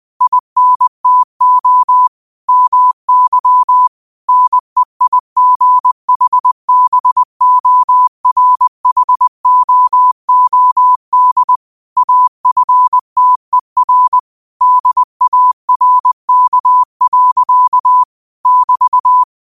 Morse code, heard as a keyed tone, about 20 wpm, 1000 Hz. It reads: INTO MY NEIGHBORHOOD AFTER DARK. <BT>